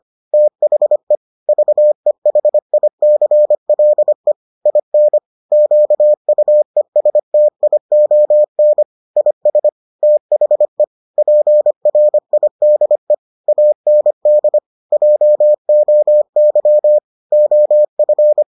Morse code, heard as a keyed tone, about 25 wpm, 600 Hz. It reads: THE VEHICLE IN QUESTION IS THE PRIDE AND JOY OF